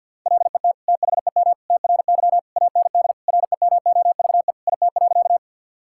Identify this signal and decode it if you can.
Morse code, heard as a keyed tone, about 50 wpm, 700 Hz. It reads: LET THEM TRY AND REMOVE IT.
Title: Sentences from Top 1000 Words - Review Part 5 - 50wpm